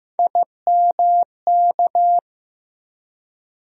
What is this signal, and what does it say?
Morse code, heard as a keyed tone, about 15 wpm, 700 Hz. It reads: IMK
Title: USA Call Sign Postfixes - 3 Letters 15wpm